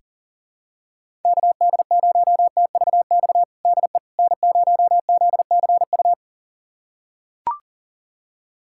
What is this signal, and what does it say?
Morse code, heard as a keyed tone, about 40 wpm, 700 Hz. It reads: KD0TVX DE N0ZCU
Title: USA Callsign DE Callsign 40wpm